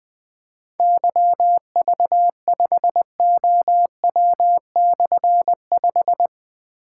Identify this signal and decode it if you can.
Morse code, heard as a keyed tone, about 21 wpm, 700 Hz. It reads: YV5OW/5
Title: Difficult Call Signs 20wpm